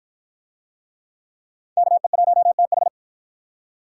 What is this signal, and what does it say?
Morse code, heard as a keyed tone, about 48 wpm, 700 Hz. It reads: KE1TH